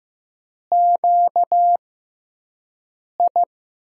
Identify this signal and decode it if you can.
Morse code, heard as a keyed tone, about 15 wpm, spaced as for 5 wpm, 700 Hz. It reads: QI
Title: USA Call Signs - 6x Character Spacing - 15wpm